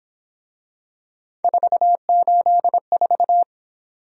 Morse code, heard as a keyed tone, about 27 wpm, 700 Hz. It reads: 484
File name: Speed-Racing - Three Letter-Number Combinations - 22wpm